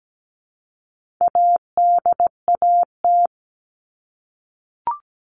Morse code, heard as a keyed tone, about 17 wpm, 700 Hz. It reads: ADAT
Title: All Cut Numbers - 4-Digit 17wpm